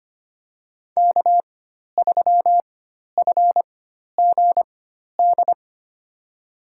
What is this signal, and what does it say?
Morse code, heard as a keyed tone, about 25 wpm, spaced as for 12 wpm, 700 Hz. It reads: K3FGD
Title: USA Call Signs - 5 Characters - 4x Character Spacing - 25wpm